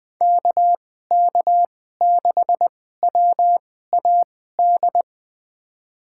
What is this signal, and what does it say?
Morse code, heard as a keyed tone, about 20 wpm, spaced as for 14 wpm, 700 Hz. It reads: KK6WAD
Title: USA Call Signs - 6 Characters - 2x Character Spacing - 20wpm